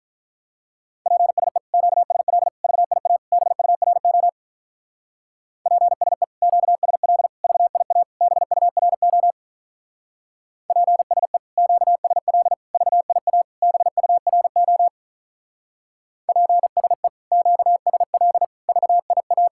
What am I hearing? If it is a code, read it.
Morse code, heard as a keyed tone, about 43 wpm, 700 Hz. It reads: PSE QSL VIA BURO PSE QSL VIA BURO PSE QSL VIA BURO PSE QSL VIA
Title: Speed-Racing Common QSO Phrases Encore - 35wpm